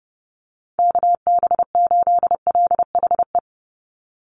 Morse code, heard as a keyed tone, about 30 wpm, 700 Hz. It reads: KB8LHE
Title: USA Call Signs 30wpm